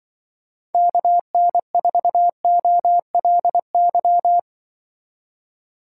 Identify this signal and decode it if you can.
Morse code, heard as a keyed tone, about 24 wpm, 700 Hz. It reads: KN4OLY
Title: Speed Racing - USA 2x3 Call Signs 20wpm